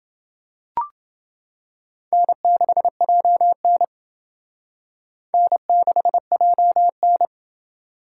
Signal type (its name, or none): Morse code